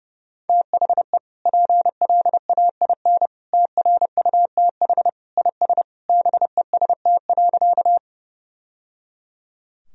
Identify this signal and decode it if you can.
Morse code, heard as a keyed tone, about 30 wpm, 700 Hz. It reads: THE PLAIN TRUTH IS BEST.
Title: Sentences from Top 1000 Words - Review Part 2 - 30wpm